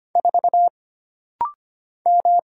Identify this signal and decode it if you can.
Morse code, heard as a keyed tone, about 25 wpm, 700 Hz. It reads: 4 M
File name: Single Character (Letter:Number:Punctuation:Prosign) - Rapid-Fire - 25wpm